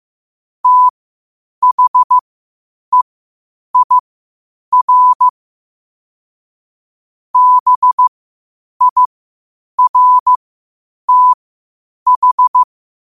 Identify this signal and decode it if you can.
Morse code, heard as a keyed tone, about 15 wpm, spaced as for 8 wpm, 1000 Hz. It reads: THEIR BIRTH
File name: qotdfiles / 8wpm